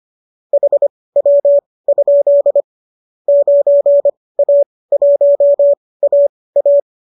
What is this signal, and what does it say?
Morse code, heard as a keyed tone, about 25 wpm, spaced as for 18 wpm, 570 Hz. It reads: HW? 9A1AA